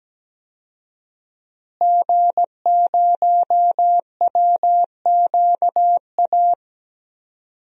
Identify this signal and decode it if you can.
Morse code, heard as a keyed tone, about 17 wpm, 700 Hz. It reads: G0WQA
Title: DXCC-100 Call Signs - 5 Characters - Repeated 2x - 17wpm